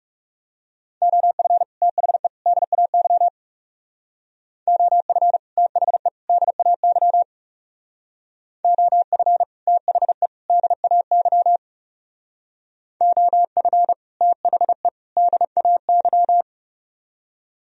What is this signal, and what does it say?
Morse code, heard as a keyed tone, about 37 wpm, 700 Hz. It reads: OF THE DAY OF THE DAY OF THE DAY OF THE DAY
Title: Speed-Racing - Sets of 3 Words 30wpm